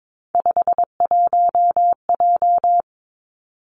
Morse code, heard as a keyed tone, about 22 wpm, 700 Hz. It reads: 51J
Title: Three Challenging Letter-Number Combinations 22wpm